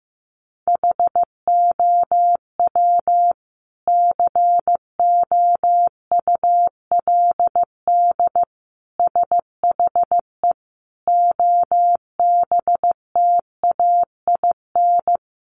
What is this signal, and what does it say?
Morse code, heard as a keyed tone, about 14 wpm, 700 Hz. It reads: HOW COULD SHE OBTAIN
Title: Sentences from Top 1300 Words 15wpm